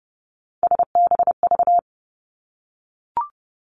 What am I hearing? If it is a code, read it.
Morse code, heard as a keyed tone, about 30 wpm, 700 Hz. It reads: SBV
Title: Three Challenging Letter-Number Combinations 30wpm